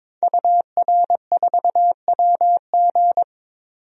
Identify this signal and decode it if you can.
Morse code, heard as a keyed tone, about 22 wpm, 700 Hz. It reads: UR4WG
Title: DXCC-100 Call Signs - 5 Characters - 22wpm